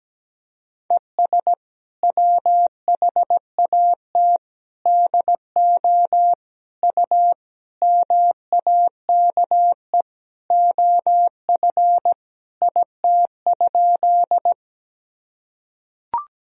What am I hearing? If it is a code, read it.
Morse code, heard as a keyed tone, about 17 wpm, 700 Hz. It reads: ES WHAT DO U MAKE OF IT?
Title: Sentences from Top 100 Words and Abbreviations 17wpm